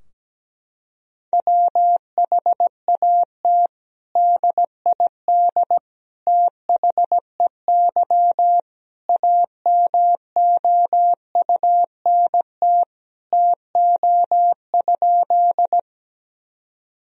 Morse code, heard as a keyed tone, about 17 wpm, 700 Hz. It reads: WHAT DID THEY AMOUNT TO?
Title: Sentences from Top 1000 Words - Review Part 4 - 17wpm